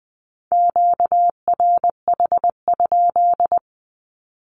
Morse code, heard as a keyed tone, about 20 wpm, 700 Hz. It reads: QRH?
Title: Q-Codes 20wpm